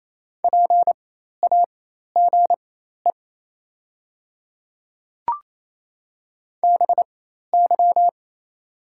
Morse code, heard as a keyed tone, about 28 wpm, spaced as for 13 wpm, 700 Hz. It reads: PAGE BY
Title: 4x Character Spacing - Top 300 Words - 28wpm